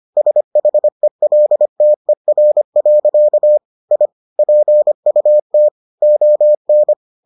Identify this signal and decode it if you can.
Morse code, heard as a keyed tone, about 25 wpm, 600 Hz. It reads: SHELTER. I PUT ON